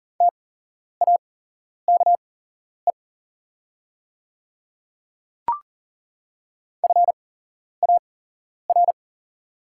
Morse code, heard as a keyed tone, about 38 wpm, spaced as for 11 wpm, 700 Hz. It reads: TAKE FAR
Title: 8x Character Spacing - Top 300 Words - 40wpm